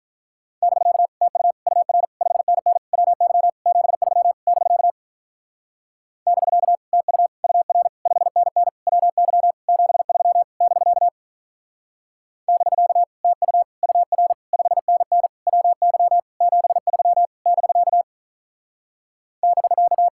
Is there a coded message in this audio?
Yes: Morse code from 0.6 to 20.1 s, heard as a keyed tone, about 44 wpm, 700 Hz. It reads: <BK> TU UR 5NN WY 73 <BK> <BK> TU UR 5NN WY 73 <BK> <BK> TU UR 5NN WY 73 <BK> <BK>